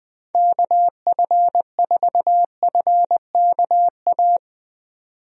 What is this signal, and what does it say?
Morse code, heard as a keyed tone, about 20 wpm, 700 Hz. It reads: KF4FKA
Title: USA Call Signs - 6 Characters - 20wpm